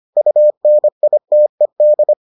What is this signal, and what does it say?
Morse code, heard as a keyed tone, about 25 wpm, 600 Hz. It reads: UNITED